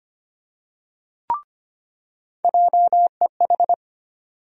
Morse code, heard as a keyed tone, about 25 wpm, 700 Hz. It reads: JEH
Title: USA Call Sign Postfixes - 3 Letters 25wpm